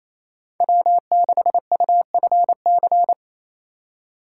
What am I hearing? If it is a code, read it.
Morse code, heard as a keyed tone, about 36 wpm, 700 Hz. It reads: W6UFC